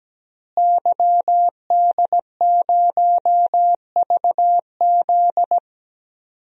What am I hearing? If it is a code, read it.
Morse code, heard as a keyed tone, about 17 wpm, 700 Hz. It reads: YD0VZ
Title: DXCC-100 Call Signs - 5 Characters - Repeated 2x - 17wpm